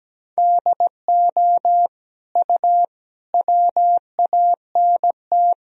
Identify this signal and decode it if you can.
Morse code, heard as a keyed tone, about 17 wpm, 700 Hz. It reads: DO U WANT
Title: Sentences from Top 400 Words and Abbreviations 17wpm